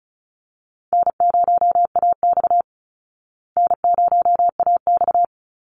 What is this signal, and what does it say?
Morse code, heard as a keyed tone, about 34 wpm, 700 Hz. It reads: N0AX N0AX
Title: Easy Call Signs Repeated 2x - 35wpm